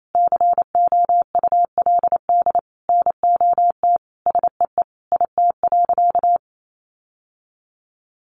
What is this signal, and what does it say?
Morse code, heard as a keyed tone, about 28 wpm, 700 Hz. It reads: COULD NOT SEE IT.
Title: Sentences from Top 1000 Words 28wpm